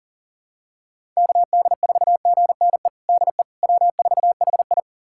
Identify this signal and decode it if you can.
Morse code, heard as a keyed tone, about 40 wpm, 700 Hz. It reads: KD4GNE DE W4HI
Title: USA Callsign DE Callsign 40wpm